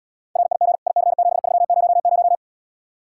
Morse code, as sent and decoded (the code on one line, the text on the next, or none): .-. .. --. / .. -.-. --... ...-- ----- -----
RIG IC7300